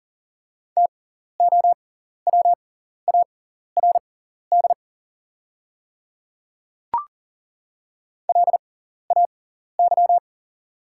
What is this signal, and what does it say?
Morse code, heard as a keyed tone, about 40 wpm, spaced as for 14 wpm, 700 Hz. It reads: TOWARD LAY
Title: 6x Character Spacing - Top 500 Words - 40wpm